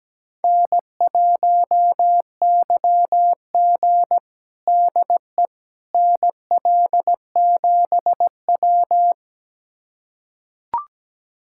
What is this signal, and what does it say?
Morse code, heard as a keyed tone, about 17 wpm, 700 Hz. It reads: N1YG DE NL7W
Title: USA Callsign DE Callsign - 4 Characters - 17wpm